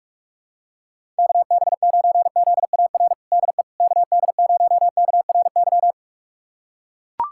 Morse code, heard as a keyed tone, about 41 wpm, 700 Hz. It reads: KB9ZAR DE KD0KRY
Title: USA Callsign DE Callsign - 6 Characters - 45wpm